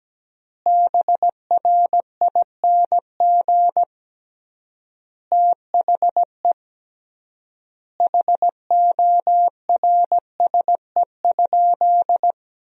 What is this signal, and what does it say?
Morse code, heard as a keyed tone, about 17 wpm, 700 Hz. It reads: BRING THE HORSE?
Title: 3x Word-Spacing - Sentences from Top 500 Words - 17wpm